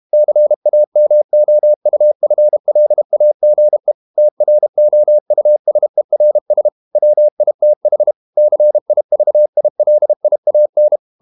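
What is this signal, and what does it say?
Morse code, heard as a keyed tone, about 32 wpm, 600 Hz. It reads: CAMOUFLAGE TROUSERS WITH CIVILIAN
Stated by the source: Morse code practice at 32 wpm from CW med Gnister, reading Ken Follett's novel Never in English.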